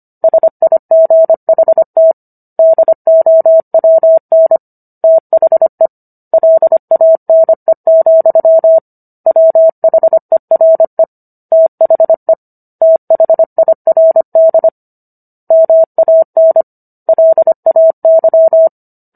Morse code, heard as a keyed tone, about 25 wpm, 650 Hz. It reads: SIGHT DOWN THE LANE, WHERE THE THIRD MAN LAY
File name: Book One / 16-Chapter 16 - 25 WPM